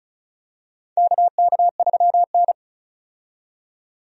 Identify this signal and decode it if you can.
Morse code, heard as a keyed tone, about 34 wpm, 700 Hz. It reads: KK3N
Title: USA Call Signs - 4 Characters - 35wpm